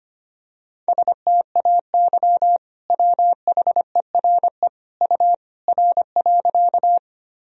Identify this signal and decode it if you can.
Morse code, heard as a keyed tone, about 25 wpm, 700 Hz. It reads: STAY WHERE U R.
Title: Sentences from Top 500 Words and Abbreviations 25wpm